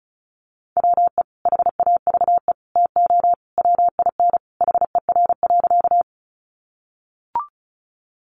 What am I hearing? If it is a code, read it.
Morse code, heard as a keyed tone, about 34 wpm, 700 Hz. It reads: WE HAVE TO WIN HER.
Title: Sentences from Top 1500 Words 35wpm